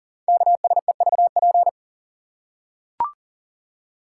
Morse code, heard as a keyed tone, about 39 wpm, 690 Hz. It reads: KSEVP
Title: Code Groups - 5 Letter:Numbers - 40wpm